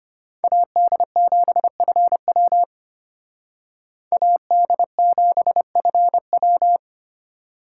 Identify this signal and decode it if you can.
Morse code, heard as a keyed tone, about 29 wpm, 700 Hz. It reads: AD7FW AD7FW
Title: Speed Racing - USA Call Signs 25wpm